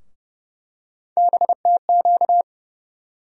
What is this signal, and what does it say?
Morse code, heard as a keyed tone, about 32 wpm, 700 Hz. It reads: BTQ
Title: USA Call Sign Postfixes - 3 Letters 30wpm